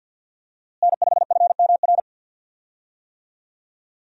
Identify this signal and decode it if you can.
Morse code, heard as a keyed tone, about 53 wpm, 700 Hz. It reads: N5FMR
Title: USA Call Signs - 5 Characters - Repeated 3x - 50wpm